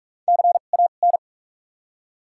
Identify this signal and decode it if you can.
Morse code, heard as a keyed tone, about 46 wpm, spaced as for 33 wpm, 700 Hz. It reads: CAN
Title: 2x Character Spacing - Top 300 Words - 45wpm